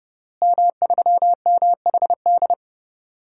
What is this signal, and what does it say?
Morse code, heard as a keyed tone, about 32 wpm, 700 Hz. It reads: M3MHD
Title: DXCC-100 Call Signs - 5 Characters - Repeated 3x - 30wpm